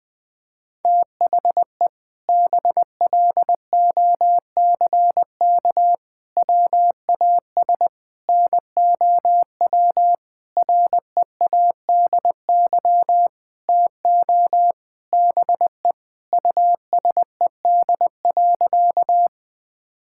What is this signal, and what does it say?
Morse code, heard as a keyed tone, about 20 wpm, 700 Hz. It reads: THE BLOCK WAS NOW READY TO BE USED.
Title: Sentences from Top 1600 Words 20wpm